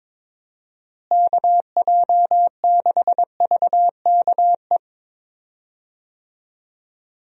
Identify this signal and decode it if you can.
Morse code, heard as a keyed tone, about 22 wpm, 700 Hz. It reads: KJ6VKE
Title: USA Call Signs - 6 Characters - Repeated 3x - 22wpm